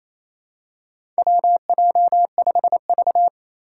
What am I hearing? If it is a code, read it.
Morse code, heard as a keyed tone, about 27 wpm, 700 Hz. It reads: WJ5V
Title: Four Challenging Letter-Number Combinations 28wpm